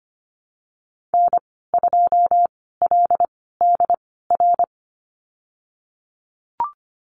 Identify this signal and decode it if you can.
Morse code, heard as a keyed tone, about 25 wpm, spaced as for 16 wpm, 700 Hz. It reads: N2LDR
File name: Lesson 15g - Number 5 - Call Signs 25wpm